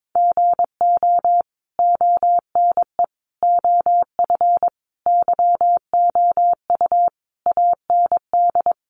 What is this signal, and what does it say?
Morse code, heard as a keyed tone, about 22 wpm, 700 Hz. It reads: GO ONE OF YOU AND
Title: Sentences from Top 1100 Words 22wpm